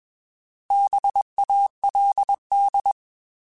Morse code, heard as a keyed tone, about 21 wpm, 800 Hz. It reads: BALD
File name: Bs @20